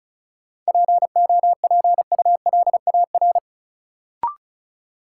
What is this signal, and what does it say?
Morse code, heard as a keyed tone, about 34 wpm, 700 Hz. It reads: POPULAR
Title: Sentences from Top 1200 Words - New Word Drill Alt - 35wpm